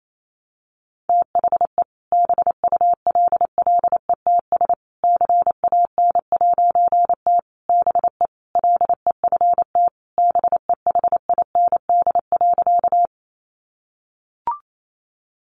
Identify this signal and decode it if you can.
Morse code, heard as a keyed tone, about 28 wpm, 700 Hz. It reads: THE BULLETS CAN'T BE LEFT BEHIND.
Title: A Sound Of Thunder 28wpm